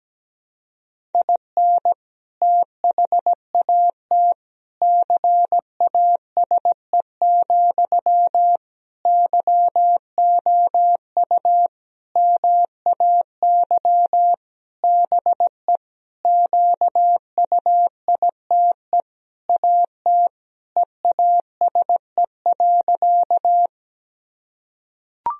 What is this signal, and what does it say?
Morse code, heard as a keyed tone, about 17 wpm, 700 Hz. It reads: IN THAT CASE, YOU MAY BE QUITE AT EASE.
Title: Sentences from Top 1000 Words - Review Part 1 - 17wpm